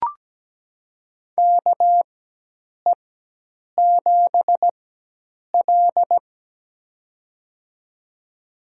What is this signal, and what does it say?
Morse code, heard as a keyed tone, about 17 wpm, spaced as for 8 wpm, 700 Hz. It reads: KE7L
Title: USA Call Signs - 4 Characters - 4x Character Spacing - 17wpm